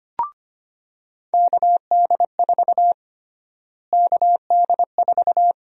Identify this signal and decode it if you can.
Morse code, heard as a keyed tone, about 25 wpm, 700 Hz. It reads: KD4 KD4
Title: USA Call Sign Prefixes - Repeated 2x - 25wpm